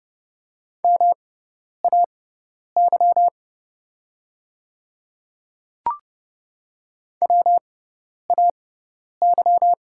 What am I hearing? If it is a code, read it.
Morse code, heard as a keyed tone, about 30 wpm, spaced as for 10 wpm, 690 Hz. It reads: MAY WAY
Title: 6x Character Spacing - Top 300 Words - 30wpm